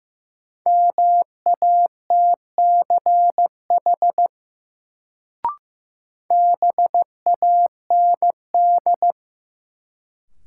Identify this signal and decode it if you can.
Morse code, heard as a keyed tone, about 15 wpm, 700 Hz. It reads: MATCH BAND
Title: Sentences from Top 1600 Words - New Word Drill Alt - 15wpm